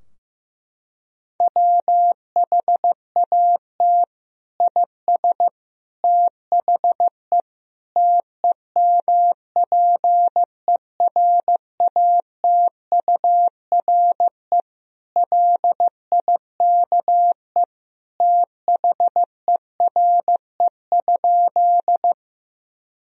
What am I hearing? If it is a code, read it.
Morse code, heard as a keyed tone, about 16 wpm, 700 Hz. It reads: WHAT IS THE TEMPERATURE LIKE THERE?
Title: Sentences from Top 1000 Words - Review Part 4 - 15wpm